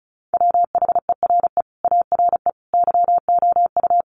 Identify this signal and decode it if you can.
Morse code, heard as a keyed tone, about 34 wpm, 700 Hz. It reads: WHERE ARE YOU
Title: Sentences from Top 1100 Words 35wpm